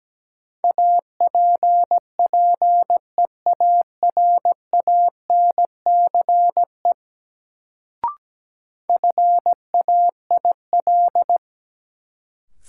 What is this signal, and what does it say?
Morse code, heard as a keyed tone, about 17 wpm, 700 Hz. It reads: APPEARANCE FAIL
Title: Sentences from Top 1100 Words - New Word Drill Alt - 17wpm